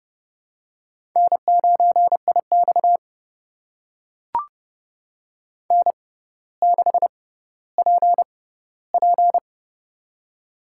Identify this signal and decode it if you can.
Morse code, heard as a keyed tone, about 30 wpm, spaced as for 15 wpm, 690 Hz. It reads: N9IX N6PP
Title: USA Call Signs - 4 Characters - 6x Character Spacing - 30wpm